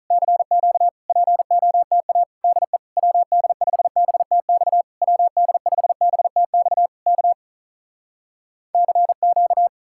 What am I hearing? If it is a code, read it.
Morse code, heard as a keyed tone, about 48 wpm, 700 Hz. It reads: CQ POTA DE WD5BTX WD5BTX K CQ